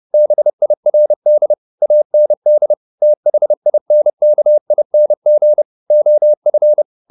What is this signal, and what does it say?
Morse code, heard as a keyed tone, about 30 wpm, 600 Hz. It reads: BIRD AND THINKING OF
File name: the old man and the sea / the old man and the sea 1 30wpm